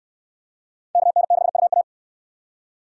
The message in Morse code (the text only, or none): KN6UA